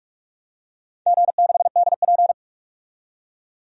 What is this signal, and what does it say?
Morse code, heard as a keyed tone, about 41 wpm, 700 Hz. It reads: G6DP
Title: DXCC-100 Call Signs - 4 Characters - 45wpm